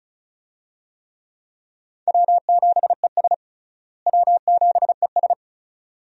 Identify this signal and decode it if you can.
Morse code, heard as a keyed tone, about 36 wpm, 700 Hz. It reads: W7ES W7ES
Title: USA Call Signs - 4 Characters - Repeated 2x - 35wpm